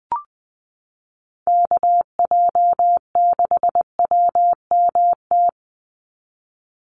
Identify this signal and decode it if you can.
Morse code, heard as a keyed tone, about 20 wpm, 700 Hz. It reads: KJ6WMT
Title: USA Call Signs 20wpm